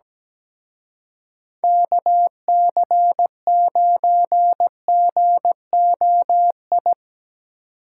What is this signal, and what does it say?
Morse code, heard as a keyed tone, about 18 wpm, 700 Hz. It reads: KC9GOI
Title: Speed Racing - USA 2x3 Call Signs 17wpm